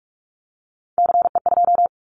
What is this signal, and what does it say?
Morse code, heard as a keyed tone, about 46 wpm, 700 Hz. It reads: CE2
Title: Callsign Prefixes for Top 100 DXCC - 45wpm